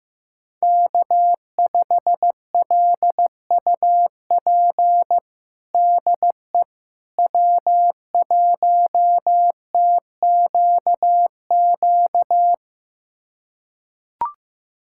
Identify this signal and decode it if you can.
Morse code, heard as a keyed tone, about 15 wpm, 700 Hz. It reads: K5LUP DE W1TQQ
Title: USA Callsign DE Callsign - 5 Characters - 15wpm